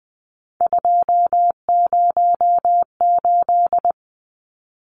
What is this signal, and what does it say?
Morse code, heard as a keyed tone, about 19 wpm, 700 Hz. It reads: 208
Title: Numbers - 3-Digit 20wpm